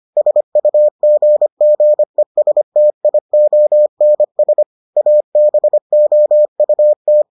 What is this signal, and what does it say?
Morse code, heard as a keyed tone, about 25 wpm, 600 Hz. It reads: SUGGESTIONS ABOUT